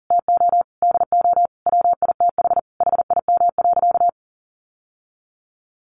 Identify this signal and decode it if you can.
Morse code, heard as a keyed tone, about 40 wpm, 700 Hz. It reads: TO DO WITH HIM.
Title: Sentences from Top 1000 Words 40wpm